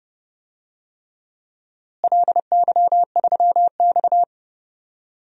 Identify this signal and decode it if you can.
Morse code, heard as a keyed tone, about 30 wpm, 700 Hz. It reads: LY3X
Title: DXCC-100 Call Signs - 4 Characters - Repeated 3x - 30wpm